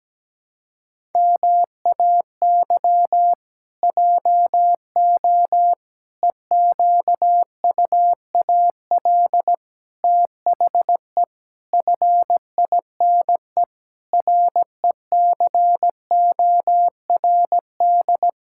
Morse code, heard as a keyed tone, about 17 wpm, 700 Hz. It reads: MAY JO EQUAL THE FINE RECORD
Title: Pangram Challenge v2 - 17wpm